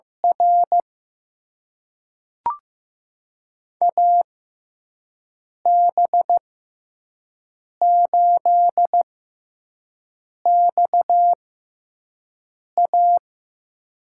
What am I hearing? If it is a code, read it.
Morse code, heard as a keyed tone, about 15 wpm, spaced as for 5 wpm, 700 Hz. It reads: R AB8XA
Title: USA Call Signs - 6x Character Spacing - 15wpm